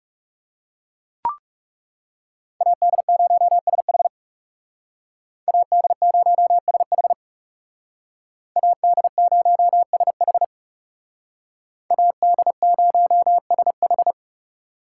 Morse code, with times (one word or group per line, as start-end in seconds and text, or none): 2.6-4.1 AD0SH
5.5-7.1 AD0SH
8.6-10.4 AD0SH
11.9-14.1 AD0SH